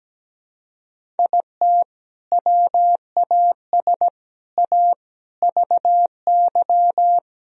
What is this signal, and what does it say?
Morse code, heard as a keyed tone, about 17 wpm, 700 Hz. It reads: IT WAS A VY
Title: Sentences from Top 500 Words and Abbreviations 17wpm